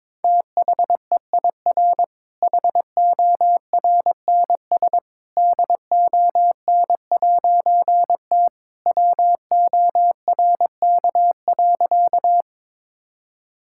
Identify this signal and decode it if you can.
Morse code, heard as a keyed tone, about 22 wpm, 700 Hz. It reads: THEIR HORNS DON'T WORK.
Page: morse-code-ninja-practice-22wpm